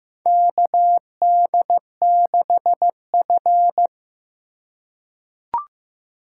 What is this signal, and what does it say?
Morse code, heard as a keyed tone, about 15 wpm, 700 Hz. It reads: KD6F
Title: USA Call Signs - 3x Character Spacing - 15wpm